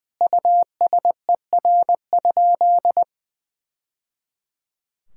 Morse code, heard as a keyed tone, about 20 wpm, 700 Hz. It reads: USER?
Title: Sentences from Top 1000 Words - Review Part 5 - 20wpm